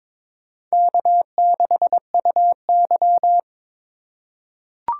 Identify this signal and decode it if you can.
Morse code, heard as a keyed tone, about 22 wpm, 700 Hz. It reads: K6UY
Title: USA Call Signs - 3x Character Spacing - 22wpm